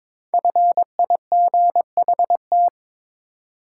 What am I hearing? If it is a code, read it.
Morse code, heard as a keyed tone, about 22 wpm, 700 Hz. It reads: FIGHT
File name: 5 Letter Words - First 3 Letters Given - 22wpm